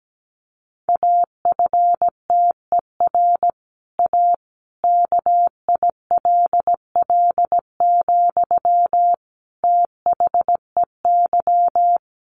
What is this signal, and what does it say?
Morse code, heard as a keyed tone, about 17 wpm, 700 Hz. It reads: AFTER A KILL, THEY
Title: Sentences from Top 1500 Words 17wpm